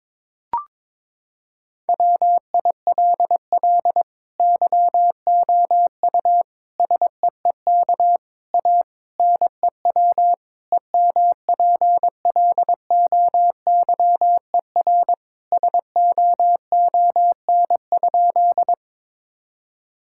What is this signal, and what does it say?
Morse code, heard as a keyed tone, about 22 wpm, 700 Hz. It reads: WILL YOU SEEK A NEW EMPLOYER SOON?
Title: Sentences from Top 1600 Words 22wpm